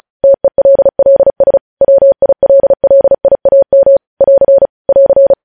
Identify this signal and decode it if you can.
Morse code, heard as a keyed tone, about 36 wpm, 550 Hz. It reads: TELLS WILLIAM <AR> <AR>
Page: news headlines in morse code at 35 wpm